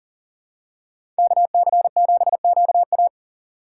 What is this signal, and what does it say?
Morse code, heard as a keyed tone, about 40 wpm, 700 Hz. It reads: KC7QA
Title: USA Call Signs - 5 Characters - 3x Character Spacing - 40wpm